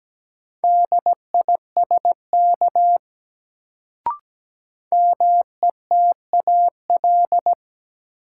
Morse code, heard as a keyed tone, about 17 wpm, 700 Hz. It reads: DISK METAL
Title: Sentences from Top 1000 Words - Review Part 6 - New Word Drill - 17wpm